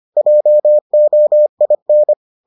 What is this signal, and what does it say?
Morse code, heard as a keyed tone, about 25 wpm, 600 Hz. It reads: JOIN